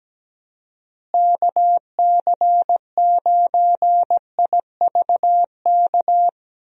Morse code, heard as a keyed tone, about 17 wpm, 700 Hz. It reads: KC9IVK
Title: USA Call Signs - 6 Characters - Repeated 3x - 17wpm